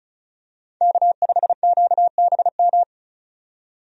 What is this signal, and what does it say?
Morse code, heard as a keyed tone, about 34 wpm, 700 Hz. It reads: K5QBM
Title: USA Call Signs - 5 Characters - 35wpm